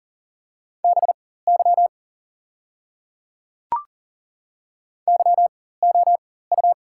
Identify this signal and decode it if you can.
Morse code, heard as a keyed tone, about 40 wpm, spaced as for 19 wpm, 690 Hz. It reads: BY YOU
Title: 4x Character Spacing - Top 200 Words - 40wpm